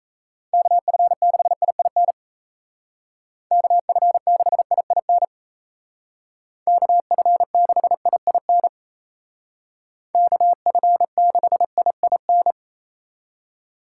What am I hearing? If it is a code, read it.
Morse code, heard as a keyed tone, about 36 wpm, 700 Hz. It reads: KF6IIN KF6IIN KF6IIN KF6IIN